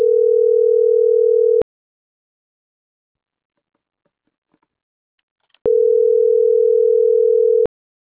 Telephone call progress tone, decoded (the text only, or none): ringback tone